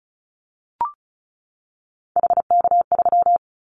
Morse code, heard as a keyed tone, about 34 wpm, 700 Hz. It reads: HK3
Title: Callsign Prefixes for Top 100 DXCC - 35wpm